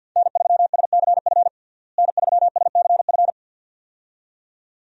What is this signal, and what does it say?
Morse code, heard as a keyed tone, about 51 wpm, 700 Hz. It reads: N3SCF N3SCF
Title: USA Call Signs - 5 Characters - Repeated 2x - 50wpm